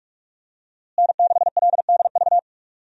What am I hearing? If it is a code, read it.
Morse code, heard as a keyed tone, about 41 wpm, 700 Hz. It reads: N6LDV
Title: USA Call Signs - 5 Characters - Repeated 2x - 45wpm